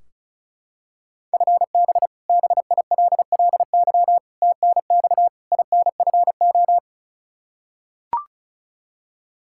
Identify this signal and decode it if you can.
Morse code, heard as a keyed tone, about 34 wpm, 700 Hz. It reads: FB BILLY TNX INFO
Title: K1USN SST QSO Practice 35wpm